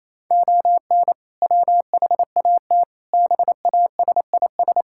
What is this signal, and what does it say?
Morse code, heard as a keyed tone, about 28 wpm, 700 Hz. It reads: ON WHAT BASIS
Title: Sentences from Top 1000 Words - Review Part 5 - 28wpm